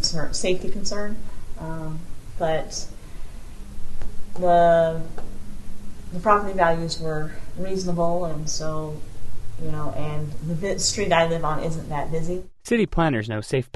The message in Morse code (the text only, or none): none